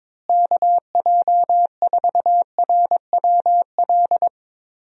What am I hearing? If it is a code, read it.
Morse code, heard as a keyed tone, about 22 wpm, 700 Hz. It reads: KJ4RWL
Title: USA Call Signs - 6 Characters - Repeated 3x - 22wpm